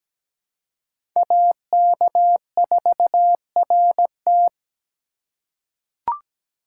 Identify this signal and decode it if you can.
Morse code, heard as a keyed tone, about 17 wpm, 700 Hz. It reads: AK4RT
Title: USA Call Signs - 3x Character Spacing - 17wpm